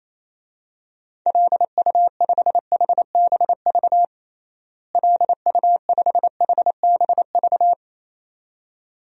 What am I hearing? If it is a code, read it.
Morse code, heard as a keyed tone, about 27 wpm, 700 Hz. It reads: LU5HBV LU5HBV
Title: DXCC-100 Call Signs - 6 Characters - Repeated 2x - 28wpm